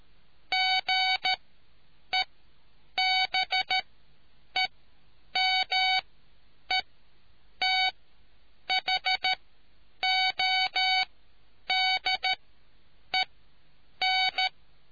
Morse code, heard as a keyed tone, about 13 wpm, spaced as for 8 wpm, 750 Hz. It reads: GEBEMETHODEN